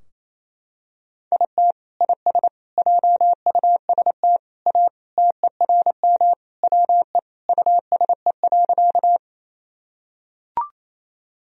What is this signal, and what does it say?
Morse code, heard as a keyed tone, about 28 wpm, 700 Hz. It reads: IT IS JUST A TERM WE USE.
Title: Sentences from Top 1000 Words - Review Part 4 - 28wpm